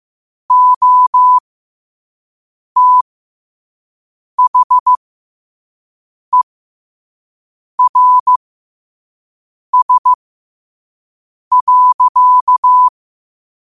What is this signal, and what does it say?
Morse code, heard as a keyed tone, about 15 wpm, spaced as for 5 wpm, 1000 Hz. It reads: OTHERS.